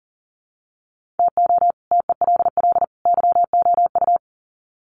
Morse code, heard as a keyed tone, about 37 wpm, 700 Hz. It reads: TO TELL YOU
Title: Sets of 3 Words 40wpm